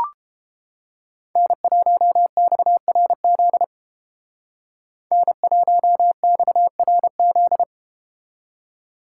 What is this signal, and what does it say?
Morse code, heard as a keyed tone, about 30 wpm, 700 Hz. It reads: N1XRZ N1XRZ